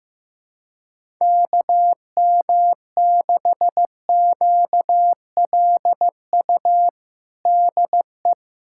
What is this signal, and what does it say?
Morse code, heard as a keyed tone, about 15 wpm, 700 Hz. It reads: KM6QLU DE
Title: USA Callsign DE Callsign 15wpm